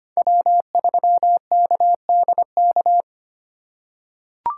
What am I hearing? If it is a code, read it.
Morse code, heard as a keyed tone, about 25 wpm, 700 Hz. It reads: W3KDK
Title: USA Call Signs - 5 Characters - 2x Character Spacing - 25wpm